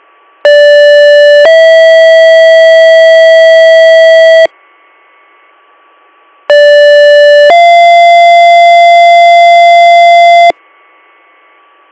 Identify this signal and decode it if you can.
Two-tone page, heard as a one-second tone, then a three-second tone, about 580 then 650 Hz, about 580 then 690 Hz, with noise. It is heard from 0.4 to 10.5 s.